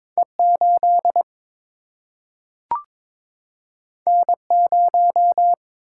Morse code, heard as a keyed tone, about 23 wpm, 700 Hz. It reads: E 8 N 0
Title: Single-Letter Single-Number 22wpm